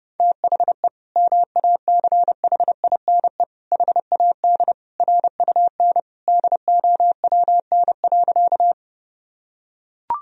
Morse code, heard as a keyed tone, about 32 wpm, 700 Hz. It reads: THE MACHINE HAD RUN DOWN.
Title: Sentences from Top 1000 Words - Review Part 2 - 30wpm